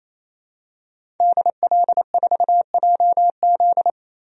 Morse code, heard as a keyed tone, about 28 wpm, 700 Hz. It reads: DL4JZ
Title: DXCC-100 Call Signs - 5 Characters - Repeated 3x - 28wpm